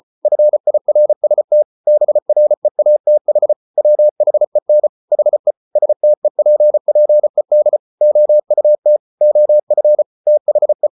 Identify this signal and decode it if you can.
Morse code, heard as a keyed tone, about 34 wpm, 600 Hz. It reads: FIRST BREATH WHEN HE STEPPED OUT OF THE